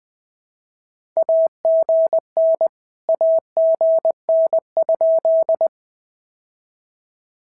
Morse code, heard as a keyed tone, about 18 wpm, 650 Hz. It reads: AGN AGN?
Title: POTA QSO Practice - 20wpm